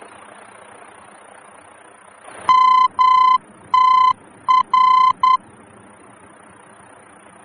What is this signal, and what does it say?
Morse code, heard as a keyed tone, about 10 wpm, 1020 Hz. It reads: MTR
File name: germany / mx / MTR-D-VOR